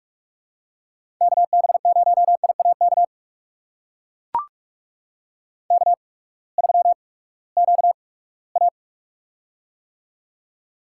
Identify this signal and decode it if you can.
Morse code, heard as a keyed tone, about 44 wpm, spaced as for 13 wpm, 700 Hz. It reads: KB0IA KK3QA
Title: USA Call Signs - 8x Character Spacing - 45wpm